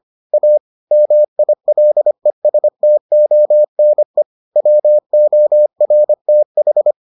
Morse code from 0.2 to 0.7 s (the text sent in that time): A